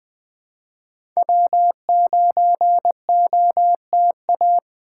Morse code, heard as a keyed tone, about 20 wpm, 700 Hz. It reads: W9OTA
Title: USA Call Signs - 5 Characters - 2x Character Spacing - 20wpm